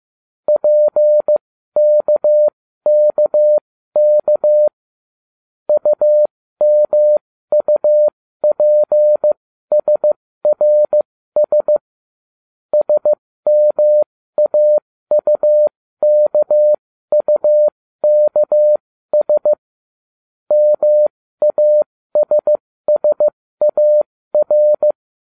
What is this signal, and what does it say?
Morse code, heard as a keyed tone, about 15 wpm, spaced as for 13 wpm, 600 Hz. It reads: PKKK UMUPSRS SMAUKUKS MASSAR